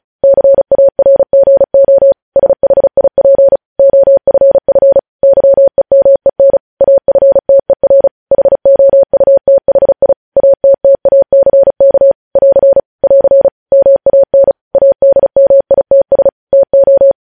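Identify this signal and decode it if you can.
Morse code, heard as a keyed tone, about 36 wpm, 550 Hz. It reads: CARGO SHIP OFF YEMEN AFTER HOUTHI ATTACK <AR> <AR> MAN ADMITS TO
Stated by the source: News headlines sent in Morse at 35 wpm.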